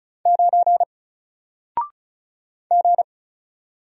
Morse code, heard as a keyed tone, about 36 wpm, 700 Hz. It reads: 9 G
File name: Single-Letter-Number-35wpm